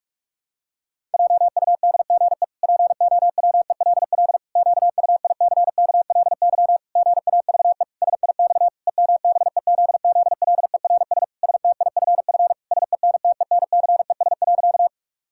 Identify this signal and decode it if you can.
Morse code, heard as a keyed tone, about 46 wpm, 700 Hz. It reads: JUDGE POWELL QUICKLY GAVE SIX EMBEZZLERS STIFF SENTENCES.
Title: Pangram Challenge v2 - 45wpm